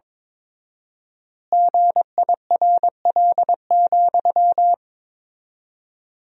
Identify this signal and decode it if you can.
Morse code, heard as a keyed tone, about 22 wpm, 700 Hz. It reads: GIRL,
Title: 4x Word-Spacing - Sentences from Top 400 Words - 22wpm